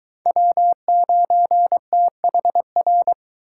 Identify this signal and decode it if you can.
Morse code, heard as a keyed tone, about 23 wpm, 700 Hz. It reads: W9THR